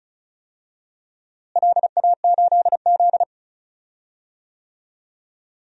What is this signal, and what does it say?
Morse code, heard as a keyed tone, about 36 wpm, 700 Hz. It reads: LA8Z
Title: DXCC-100 Call Signs - 4 Characters - Repeated 2x - 35wpm